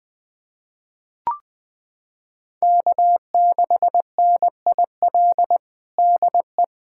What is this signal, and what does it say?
Morse code, heard as a keyed tone, about 21 wpm, 700 Hz. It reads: K6NIL DE
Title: USA Callsign DE Callsign 20wpm